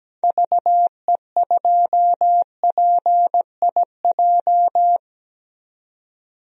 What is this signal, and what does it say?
Morse code, heard as a keyed tone, about 17 wpm, 700 Hz. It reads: VE2PIJ
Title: DXCC-100 Call Signs - 6 Characters - 17wpm